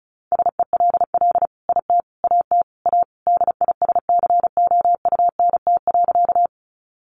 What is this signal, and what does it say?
Morse code, heard as a keyed tone, about 34 wpm, 700 Hz. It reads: SELL IT AT A DISCOUNT.
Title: Sentences from Top 1200 Words 35wpm